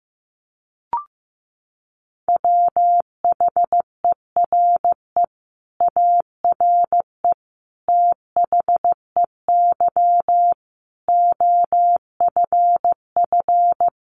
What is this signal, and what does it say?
Morse code, heard as a keyed tone, about 15 wpm, 700 Hz. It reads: WHERE ARE THEY OFF
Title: Sentences from Top 200 Words 15wpm